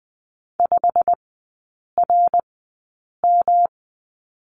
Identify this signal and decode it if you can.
Morse code, heard as a keyed tone, about 20 wpm, spaced as for 8 wpm, 690 Hz. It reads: 5RM
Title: 212 Lesson 34e - Number 8 - Three Characters 20wpm